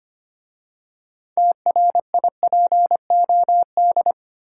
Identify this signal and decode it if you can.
Morse code, heard as a keyed tone, about 25 wpm, 700 Hz. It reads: TRIPOD
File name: practice / 25wpm / Easy Words with Hints - 25wpm